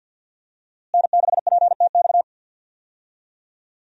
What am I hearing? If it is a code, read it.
Morse code, heard as a keyed tone, about 52 wpm, 700 Hz. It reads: N6PTX
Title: USA Call Signs - 5 Characters - Repeated 3x - 50wpm